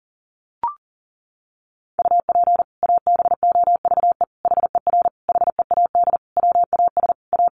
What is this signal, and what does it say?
Morse code, heard as a keyed tone, about 40 wpm, 700 Hz. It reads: UP ABOVE HER HEAD WAS A
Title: Sentences from Top 1200 Words 40wpm